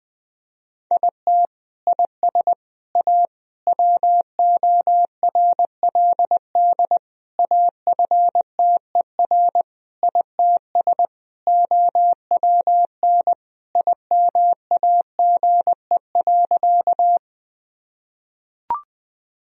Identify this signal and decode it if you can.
Morse code, heard as a keyed tone, about 20 wpm, 700 Hz. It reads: IT IS A WORLD AFTER ITS OWN IMAGE.
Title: Sentences from Top 1000 Words - Review Part 4 - 20wpm